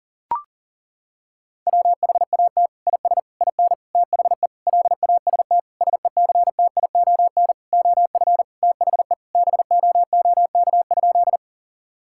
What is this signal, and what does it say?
Morse code, heard as a keyed tone, about 37 wpm, 700 Hz. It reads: WHAT IS IN THE LAST SECTION OF THE BOOK?
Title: Sentences from Top 1000 Words - Review Part 5 - 40wpm